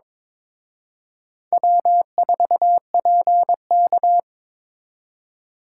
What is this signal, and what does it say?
Morse code, heard as a keyed tone, about 26 wpm, 700 Hz. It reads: W4PK
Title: Speed Racing - USA 1x2 Call Signs 22wpm